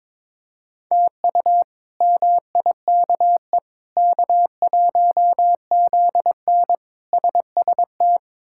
Morse code, heard as a keyed tone, about 22 wpm, 700 Hz. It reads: TU MIKE K1ZN SST
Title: K1USN SST QSO Practice - 22wpm